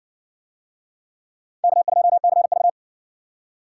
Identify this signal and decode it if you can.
Morse code, heard as a keyed tone, about 60 wpm, 700 Hz. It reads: K2CV